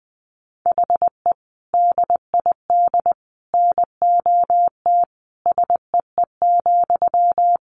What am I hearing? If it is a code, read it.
Morse code, heard as a keyed tone, about 20 wpm, 700 Hz. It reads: HE DID NOT SEE,
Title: Sentences from Top 1300 Words 20wpm